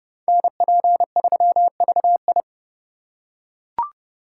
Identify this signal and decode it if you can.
Morse code, heard as a keyed tone, about 27 wpm, 700 Hz. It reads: NP3VI